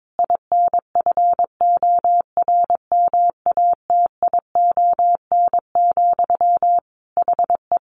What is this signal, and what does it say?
Morse code, heard as a keyed tone, about 22 wpm, 700 Hz. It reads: INFORMATION, HE
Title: Sentences from Top 600 Words 22wpm